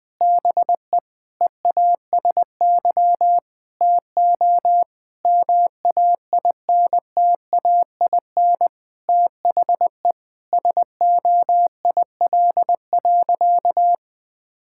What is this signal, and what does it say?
Morse code, heard as a keyed tone, about 21 wpm, 700 Hz. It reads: BE EASY TO MAINTAIN THE SOIL.
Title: Sentences from Top 1000 Words - Review Part 5 - 20wpm